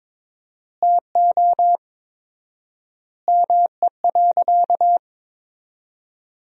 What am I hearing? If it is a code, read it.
Morse code, heard as a keyed tone, about 22 wpm, 700 Hz. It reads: TO ME.
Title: 4x Word-Spacing - Sentences from Top 400 Words - 22wpm